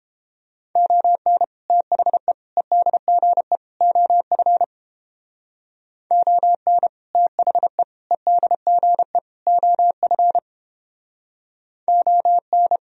Morse code, heard as a keyed tone, about 29 wpm, 700 Hz. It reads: ON THE EDGE OF ON THE EDGE OF ON